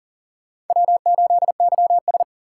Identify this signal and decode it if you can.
Morse code, heard as a keyed tone, about 40 wpm, 700 Hz. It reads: W8YS
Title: USA Call Signs - 4 Characters - 6x Character Spacing - 40wpm